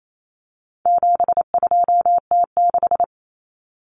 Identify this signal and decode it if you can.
Morse code, heard as a keyed tone, about 28 wpm, 700 Hz. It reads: 72T6
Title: Common Cut Numbers - 4-Digit 28wpm